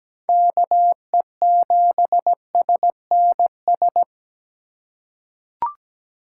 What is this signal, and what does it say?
Morse code, heard as a keyed tone, about 17 wpm, 700 Hz. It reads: KE7SNS
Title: USA Call Signs - 6 Characters - 4x Character Spacing - 17wpm